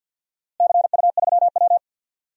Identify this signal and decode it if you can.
Morse code, heard as a keyed tone, about 52 wpm, 700 Hz. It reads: CU3W
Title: DXCC-100 Call Signs - 4 Characters - Repeated 3x - 50wpm